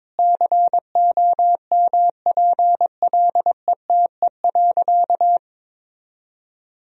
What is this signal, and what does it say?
Morse code, heard as a keyed tone, about 22 wpm, 700 Hz. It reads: COMPLETE.
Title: Sentences from Top 1000 Words - Review Part 2 - 22wpm